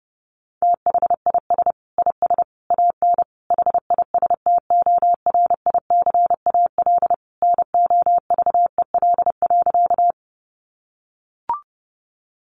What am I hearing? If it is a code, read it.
Morse code, heard as a keyed tone, about 30 wpm, 700 Hz. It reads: THIS IS AN HISTORICAL NOVEL.
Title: Sentences from Top 1300 Words 30wpm